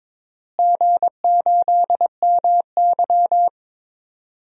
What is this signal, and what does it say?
Morse code, heard as a keyed tone, about 22 wpm, 700 Hz. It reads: G8MY
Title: DXCC-100 Call Signs - 4 Characters - Repeated 3x - 22wpm